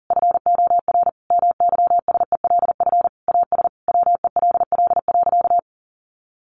Morse code, heard as a keyed tone, about 40 wpm, 700 Hz. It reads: FOR MYSELF AS WELL.
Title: Sentences from Top 1300 Words 40wpm